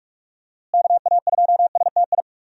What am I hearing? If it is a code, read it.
Morse code, heard as a keyed tone, about 41 wpm, 700 Hz. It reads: KA2STI
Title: USA Call Signs - 6 Characters - Repeated 2x - 45wpm